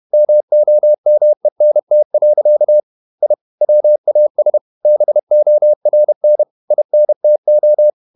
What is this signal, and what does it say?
Morse code, heard as a keyed tone, about 31 wpm, 600 Hz. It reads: MOMENT. I WAS BORN INTO
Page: Never 31wpm | CW med Gnister